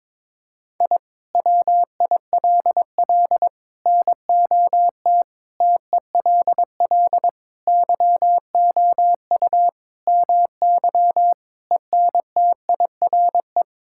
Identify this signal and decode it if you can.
Morse code, heard as a keyed tone, about 22 wpm, 700 Hz. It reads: I WILL NOT TELL YOU MY ENTIRE
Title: Sentences from Top 1000 Words - Review Part 5 - 22wpm